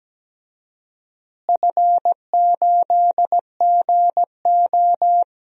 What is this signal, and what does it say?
Morse code, heard as a keyed tone, about 17 wpm, 700 Hz. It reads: F8GO
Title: DXCC-100 Call Signs - 4 Characters - Repeated 3x - 17wpm